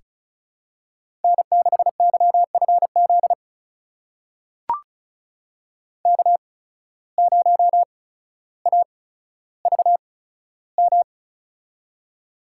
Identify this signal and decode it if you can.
Morse code, heard as a keyed tone, about 36 wpm, spaced as for 15 wpm, 700 Hz. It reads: N6YFZ K0AVM
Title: USA Call Signs - 5 Characters - 8x Character Spacing - 35wpm